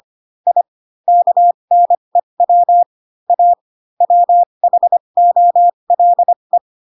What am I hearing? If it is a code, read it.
Morse code, heard as a keyed tone, about 25 wpm, spaced as for 22 wpm, 700 Hz. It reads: I KNEW A WHOLE
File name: Pinocchio / Pinocchio ch3 18 25 ik1hge